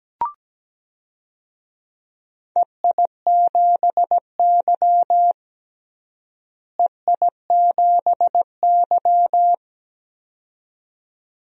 Morse code, heard as a keyed tone, about 17 wpm, 700 Hz. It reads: EI7Y EI7Y
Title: DXCC-100 Call Signs - 4 Characters - Repeated 2x - 17wpm